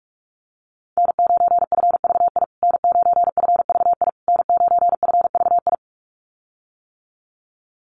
Morse code, heard as a keyed tone, about 44 wpm, 700 Hz. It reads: N9FVI N9FVI N9FVI
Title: USA Call Signs Repeated 3x 45wpm